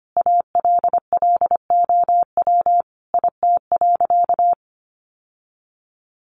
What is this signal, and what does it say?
Morse code, heard as a keyed tone, about 25 wpm, 700 Hz. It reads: ALLOW IT.
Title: Sentences from Top 700 Words 25wpm